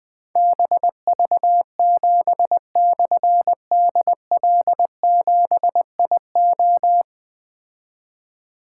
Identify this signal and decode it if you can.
Morse code, heard as a keyed tone, about 20 wpm, 700 Hz. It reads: BV7/DL7IO
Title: Difficult Call Signs 20wpm